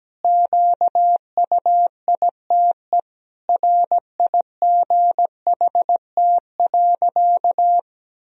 Morse code, heard as a keyed tone, about 17 wpm, 700 Hz. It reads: QUITE RIGHT.
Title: Sentences from Top 1000 Words - Review Part 2 - 17wpm